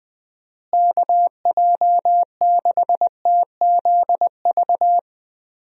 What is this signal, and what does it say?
Morse code, heard as a keyed tone, about 21 wpm, 700 Hz. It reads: KJ6TZV
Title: USA Call Signs - 6 Characters - Repeated 2x - 20wpm